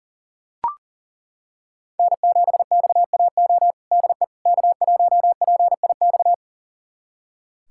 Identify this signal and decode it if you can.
Morse code, heard as a keyed tone, about 40 wpm, 700 Hz. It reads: N7XAO DE K1PIX
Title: USA Callsign DE Callsign - 5 Characters - 40wpm